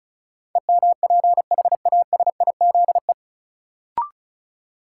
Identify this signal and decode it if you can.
Morse code, heard as a keyed tone, about 34 wpm, 700 Hz. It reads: EMPHASIZE
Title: Sentences from Top 1500 Words - New Word Drill - 35wpm